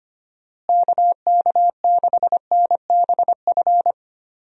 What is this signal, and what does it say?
Morse code, heard as a keyed tone, about 25 wpm, 700 Hz. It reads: KK6NBF
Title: USA Call Signs - 6 Characters - Repeated 2x - 25wpm